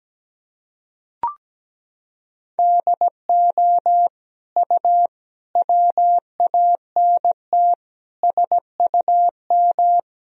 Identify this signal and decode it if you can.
Morse code, heard as a keyed tone, about 17 wpm, 700 Hz. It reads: DO U WANT SUM
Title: Sentences from Top 300 Words and Abbreviations 17wpm